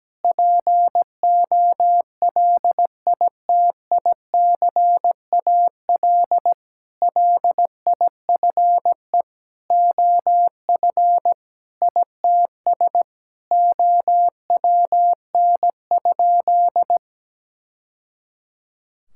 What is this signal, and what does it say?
Morse code, heard as a keyed tone, about 17 wpm, 700 Hz. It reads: POLITICAL LIFE OF ITS OWN?
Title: Sentences from Top 1000 Words - Review Part 5 - 17wpm